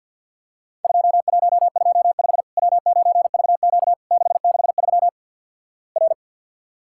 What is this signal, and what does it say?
Morse code, heard as a keyed tone, about 50 wpm, 700 Hz. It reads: 2125 W9VQ 663 R
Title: Worked All Europe DX Contest - QTC Exchange - 50wpm